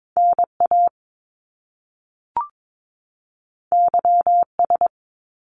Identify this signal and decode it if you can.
Morse code, heard as a keyed tone, about 22 wpm, 700 Hz. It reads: NA YS
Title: Top Two Letter Combinations in Words 22wpm